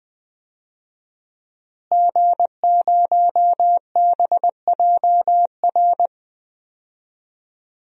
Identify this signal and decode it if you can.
Morse code, heard as a keyed tone, about 20 wpm, 700 Hz. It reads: G0BJR
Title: DXCC-100 Call Signs - 5 Characters - Repeated 2x - 20wpm